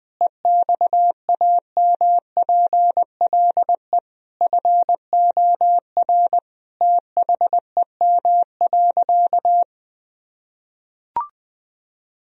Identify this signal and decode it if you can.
Morse code, heard as a keyed tone, about 20 wpm, 700 Hz. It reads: EXAMPLE FOR THEM.
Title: Sentences from Top 1000 Words - Review Part 1 - 20wpm